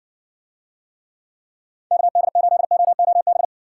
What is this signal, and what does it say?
Morse code, heard as a keyed tone, about 57 wpm, 700 Hz. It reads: KD8OOB